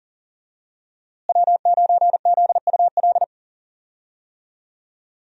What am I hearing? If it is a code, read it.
Morse code, heard as a keyed tone, about 37 wpm, 700 Hz. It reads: W9ZUL
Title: USA Call Signs - 5 Characters - Repeated 3x - 40wpm